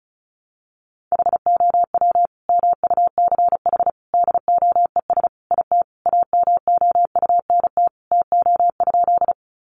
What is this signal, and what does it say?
Morse code, heard as a keyed tone, about 34 wpm, 700 Hz. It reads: HOW MUCH DOES IT AMOUNT TO?
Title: Sentences from Top 700 Words 35wpm